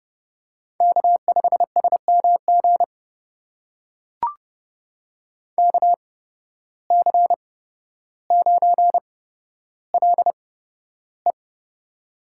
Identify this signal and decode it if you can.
Morse code, heard as a keyed tone, about 31 wpm, spaced as for 13 wpm, 700 Hz. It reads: K5SMG KC9LE
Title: USA Call Signs - 5 Characters - 8x Character Spacing - 30wpm